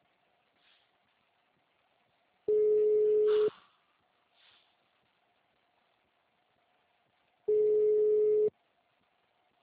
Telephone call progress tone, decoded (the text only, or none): ringback tone